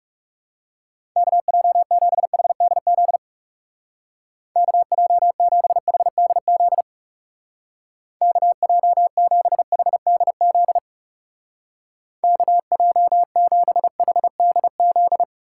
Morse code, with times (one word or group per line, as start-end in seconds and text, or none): 1.2-3.2 KJ7HDZ
4.6-6.8 KJ7HDZ
8.2-10.8 KJ7HDZ
12.2-15.2 KJ7HDZ